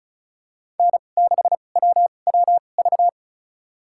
Morse code, heard as a keyed tone, about 36 wpm, spaced as for 25 wpm, 700 Hz. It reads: N6WWV
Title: USA Call Signs - 5 Characters - 2x Character Spacing - 35wpm